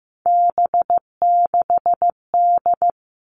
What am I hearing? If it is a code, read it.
Morse code, heard as a keyed tone, about 15 wpm, 700 Hz. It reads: B6D
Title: All Cut Numbers - 3-Digit 15wpm